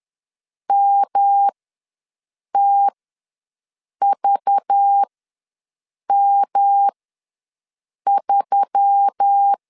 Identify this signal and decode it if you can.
Morse code, heard as a keyed tone, about 10 wpm, spaced as for 6 wpm, 800 Hz. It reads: MTVM3
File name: LEZIONI / 8 YJ45 / 8 YJ ws 10wpm 4043